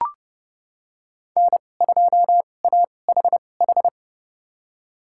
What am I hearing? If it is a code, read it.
Morse code, heard as a keyed tone, about 30 wpm, spaced as for 22 wpm, 700 Hz. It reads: N2AHH
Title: USA Call Signs - 5 Characters - 2x Character Spacing - 30wpm